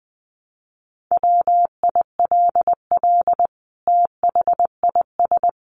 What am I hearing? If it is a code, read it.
Morse code, heard as a keyed tone, about 20 wpm, 700 Hz. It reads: WILL THIS
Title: Sentences from Top 1200 Words 20wpm